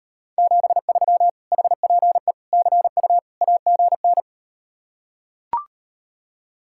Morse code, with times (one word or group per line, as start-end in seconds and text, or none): 0.4-1.3 73
1.5-2.3 HPE
2.5-3.2 CU
3.4-4.2 AGN